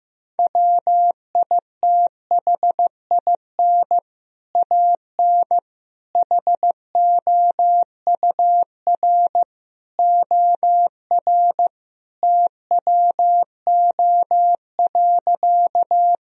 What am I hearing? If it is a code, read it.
Morse code, heard as a keyed tone, about 15 wpm, 700 Hz. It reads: WITHIN AN HOUR OR TWO.
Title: Sentences from Top 1000 Words - Review Part 4 - 15wpm